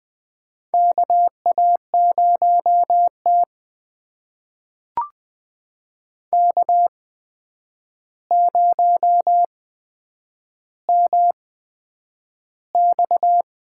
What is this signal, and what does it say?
Morse code, heard as a keyed tone, about 20 wpm, spaced as for 9 wpm, 700 Hz. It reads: KA0T K0MX
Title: USA Call Signs - 4 Characters - 8x Character Spacing - 20wpm